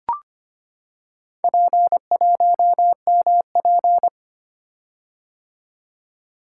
Morse code, heard as a keyed tone, about 25 wpm, 700 Hz. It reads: P1MP
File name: practice / 25wpm / Four Letter-Number Combinations - 25wpm